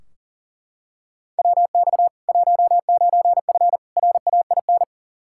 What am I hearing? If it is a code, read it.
Morse code, heard as a keyed tone, about 37 wpm, 700 Hz. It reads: WX 19F RAIN
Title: K1USN SST QSO Practice 40wpm